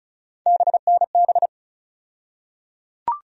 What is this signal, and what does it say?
Morse code, heard as a keyed tone, about 35 wpm, 700 Hz. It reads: BNB
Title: USA Call Sign Postfixes - 3 Letters 35wpm